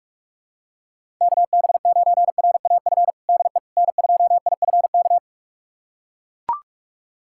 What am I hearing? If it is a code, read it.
Morse code, heard as a keyed tone, about 41 wpm, 700 Hz. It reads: KB9RAF DE N2IFK
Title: USA Callsign DE Callsign 45wpm